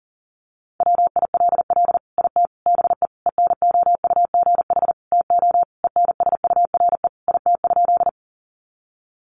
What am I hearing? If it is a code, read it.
Morse code, heard as a keyed tone, about 40 wpm, 700 Hz. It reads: WILL IT BE ENOUGH TO ENSURE IT?
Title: Sentences from Top 1100 Words 40wpm